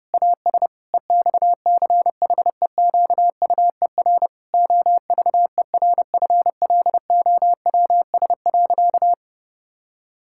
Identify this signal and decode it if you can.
Morse code, heard as a keyed tone, about 32 wpm, 700 Hz. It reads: AS EXCHEQUER OVERFLOWS.
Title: Pangram Challenge v2 - 30wpm